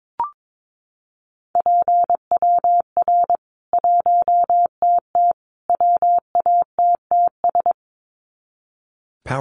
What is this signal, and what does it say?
Morse code, heard as a keyed tone, about 22 wpm, 700 Hz. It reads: PWR 1TT WATTS
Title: Common QSO Phrases 22wpm